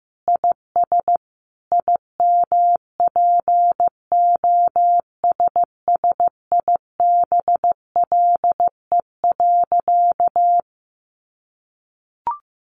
Morse code, heard as a keyed tone, about 15 wpm, 700 Hz. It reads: IS IMPOSSIBLE.
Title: Sentences from Top 1100 Words 15wpm